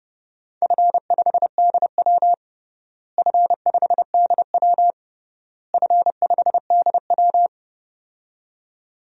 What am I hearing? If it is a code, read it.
Morse code, heard as a keyed tone, about 30 wpm, 700 Hz. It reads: F5DW F5DW F5DW
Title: DXCC-100 Call Signs - 4 Characters - Repeated 3x - 30wpm